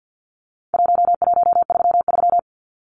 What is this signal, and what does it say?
Morse code, heard as a keyed tone, about 50 wpm, 700 Hz. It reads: 1133
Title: Numbers - Contest Style - 4-Digit 50wpm